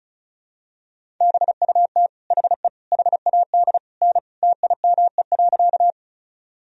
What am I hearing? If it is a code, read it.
Morse code, heard as a keyed tone, about 36 wpm, 700 Hz. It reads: BUT HE HAD N TIME.
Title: Sentences from Top 100 Words and Abbreviations 35wpm